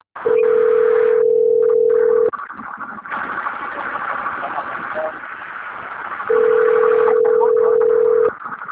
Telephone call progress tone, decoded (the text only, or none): ringback tone